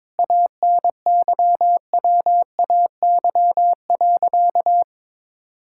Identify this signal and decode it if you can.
Morse code, heard as a keyed tone, about 22 wpm, 700 Hz. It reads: ANYWAY.
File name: practice / 22wpm / Sentences with 4-Word Phrases - 22wpm